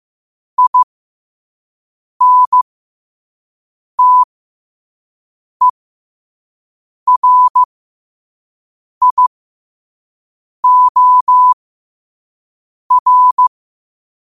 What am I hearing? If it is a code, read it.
Morse code, heard as a keyed tone, about 15 wpm, spaced as for 5 wpm, 1000 Hz. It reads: INTERIOR
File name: qotdfiles / 5wpm